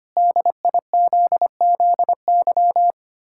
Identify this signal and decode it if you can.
Morse code, heard as a keyed tone, about 25 wpm, 700 Hz. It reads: DIZZY
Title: Pangram Challenge v2 - 25wpm